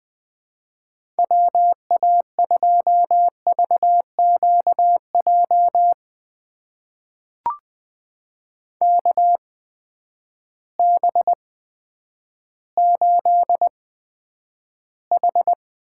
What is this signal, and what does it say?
Morse code, heard as a keyed tone, about 20 wpm, spaced as for 10 wpm, 700 Hz. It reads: WA2VQJ KB8H
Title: USA Call Signs - 6 Characters - 8x Character Spacing - 20wpm